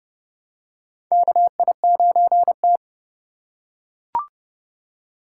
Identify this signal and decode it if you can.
Morse code, heard as a keyed tone, about 32 wpm, 700 Hz. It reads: KI9T
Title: USA Call Signs - 2x Character Spacing - 30wpm